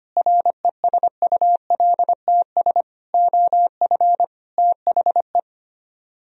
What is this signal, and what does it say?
Morse code, heard as a keyed tone, about 30 wpm, 700 Hz. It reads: RESULTS OF THE
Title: Speed-Racing - Sets of 4 Words 25wpm